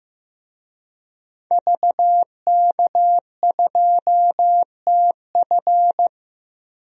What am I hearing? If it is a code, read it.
Morse code, heard as a keyed tone, about 15 wpm, 700 Hz. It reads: VK2TF
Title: DXCC-100 Call Signs - 5 Characters - Repeated 3x - 15wpm